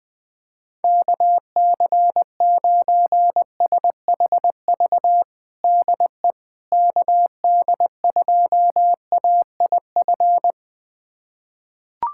Morse code, heard as a keyed tone, about 20 wpm, 700 Hz. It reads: KC9SHV DE KD2AIF
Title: USA Callsign DE Callsign - 6 Characters - 20wpm